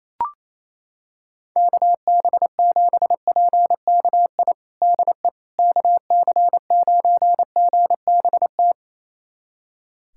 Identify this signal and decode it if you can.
Morse code, heard as a keyed tone, about 28 wpm, 700 Hz. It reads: KB7PKI DE KC9GBT
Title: USA Callsign DE Callsign - 6 Characters - 28wpm